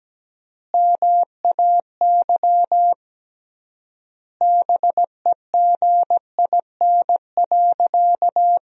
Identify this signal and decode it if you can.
Morse code, heard as a keyed tone, about 17 wpm, 700 Hz. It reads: MAY BEGIN.
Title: 3x Word-Spacing - Sentences from Top 300 Words - 17wpm